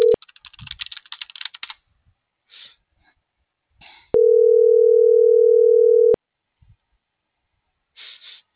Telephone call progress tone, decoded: ringback tone

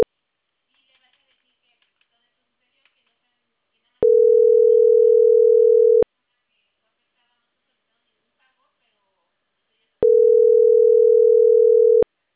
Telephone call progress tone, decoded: ringback tone

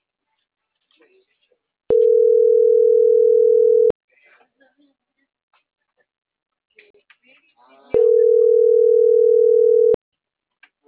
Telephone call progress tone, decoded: ringback tone

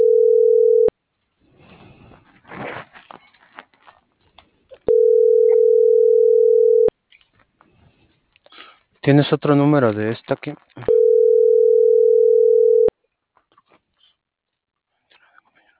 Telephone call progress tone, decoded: ringback tone

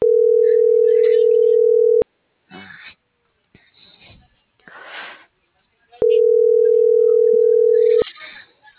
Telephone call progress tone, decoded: ringback tone

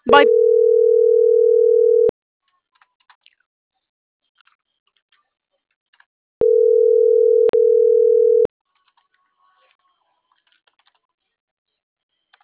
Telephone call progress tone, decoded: ringback tone

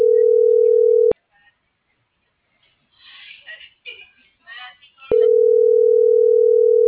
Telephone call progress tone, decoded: ringback tone

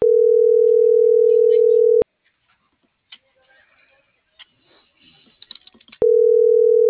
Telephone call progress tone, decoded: ringback tone

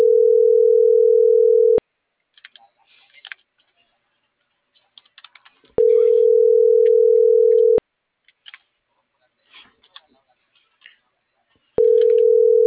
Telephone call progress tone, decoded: ringback tone

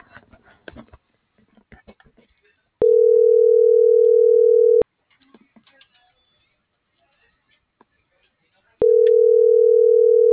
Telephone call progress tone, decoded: ringback tone

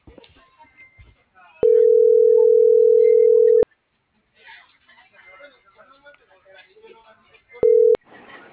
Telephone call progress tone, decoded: ringback tone

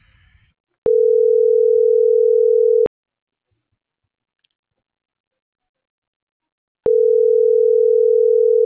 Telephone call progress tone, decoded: ringback tone

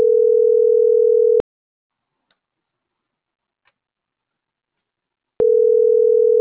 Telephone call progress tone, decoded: ringback tone

